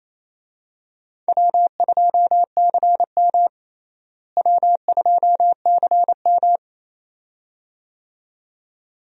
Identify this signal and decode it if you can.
Morse code, heard as a keyed tone, about 28 wpm, 700 Hz. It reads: W2CM W2CM
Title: USA Call Signs - 4 Characters - Repeated 2x - 28wpm